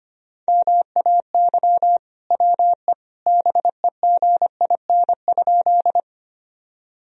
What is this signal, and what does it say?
Morse code, heard as a keyed tone, about 25 wpm, 700 Hz. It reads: MAY WE BEGIN?
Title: Sentences from Top 1000 Words - Review Part 1 - 25wpm